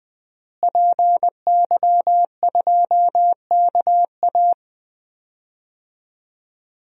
Morse code, heard as a keyed tone, about 20 wpm, 700 Hz. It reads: PY2KA
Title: DXCC-100 Call Signs - 5 Characters - Repeated 2x - 20wpm